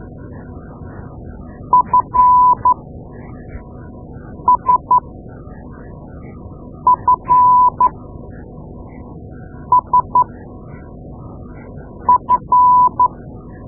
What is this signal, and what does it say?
Morse code, heard as a keyed tone, about 12 wpm, spaced as for 4 wpm, 1000 Hz. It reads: FSFSF
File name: Morse Lessons / Morse Lessons / Tape-2A